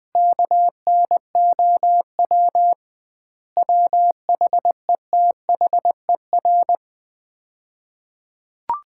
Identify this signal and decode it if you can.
Morse code, heard as a keyed tone, about 20 wpm, 700 Hz. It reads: KNOW WHETHER
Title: 2x Word-Spacing - Sets of 5 Words - 20wpm